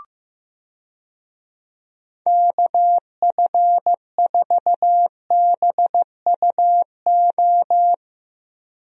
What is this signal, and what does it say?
Morse code, heard as a keyed tone, about 15 wpm, 700 Hz. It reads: KF4BUO
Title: USA Call Signs - 6 Characters - Repeated 3x - 15wpm